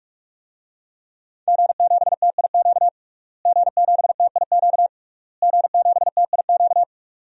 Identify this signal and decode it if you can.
Morse code, heard as a keyed tone, about 44 wpm, 700 Hz. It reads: G7TIQ G7TIQ G7TIQ
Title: DXCC-100 Call Signs - 5 Characters - Repeated 3x - 45wpm